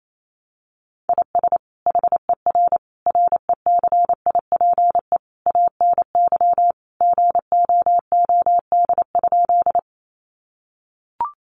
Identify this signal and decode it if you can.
Morse code, heard as a keyed tone, about 28 wpm, 700 Hz. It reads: IS HER RECIPE ANY GOOD?
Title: Sentences from Top 1200 Words 28wpm